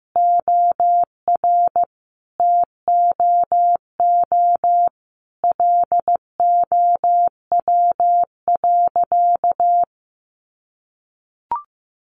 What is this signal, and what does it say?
Morse code, heard as a keyed tone, about 15 wpm, 700 Hz. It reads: OR TOO LOW.
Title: Sentences from Top 1200 Words 15wpm